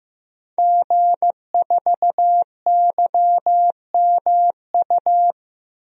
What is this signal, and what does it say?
Morse code, heard as a keyed tone, about 15 wpm, 700 Hz. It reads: G4YMU
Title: DXCC-100 Call Signs - 5 Characters - Repeated 2x - 15wpm